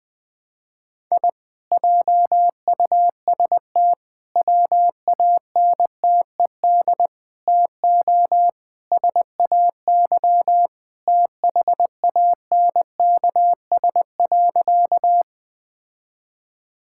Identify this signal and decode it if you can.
Morse code, heard as a keyed tone, about 21 wpm, 700 Hz. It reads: I JUST WANTED TO SAY THANKS.
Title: Sentences with 4-Word Phrases - 20wpm